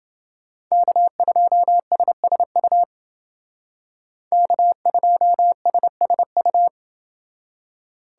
Morse code, heard as a keyed tone, about 30 wpm, 700 Hz. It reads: K2SSU K2SSU